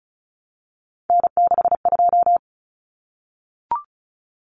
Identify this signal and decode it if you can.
Morse code, heard as a keyed tone, about 34 wpm, 700 Hz. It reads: N62
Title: Common Cut Numbers - 3-Digit 35wpm